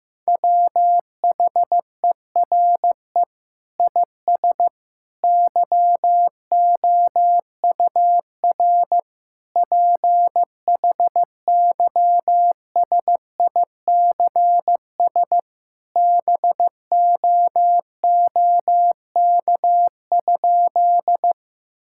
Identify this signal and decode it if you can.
Morse code, heard as a keyed tone, about 15 wpm, 700 Hz. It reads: WHERE IS YOUR PHYSICS BOOK?
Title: Sentences from Top 1000 Words - Review Part 5 - 15wpm